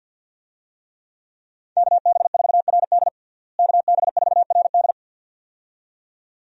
Morse code, heard as a keyed tone, about 52 wpm, 700 Hz. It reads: KB4RD KB4RD
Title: USA Call Signs - 5 Characters - Repeated 2x - 50wpm